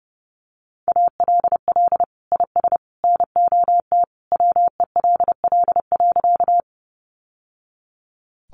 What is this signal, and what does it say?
Morse code, heard as a keyed tone, about 28 wpm, 700 Hz. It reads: ALL IS NOT WELL.
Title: Sentences from Top 600 Words 30wpm